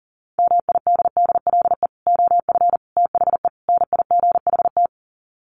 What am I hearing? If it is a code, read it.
Morse code, heard as a keyed tone, about 40 wpm, 700 Hz. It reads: MIDDLE OF THE NIGHT
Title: Sets of 4 Words 40wpm